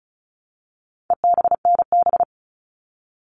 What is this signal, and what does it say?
Morse code, heard as a keyed tone, about 34 wpm, 700 Hz. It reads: EBNB
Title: All Cut Numbers - 4-Digit 35wpm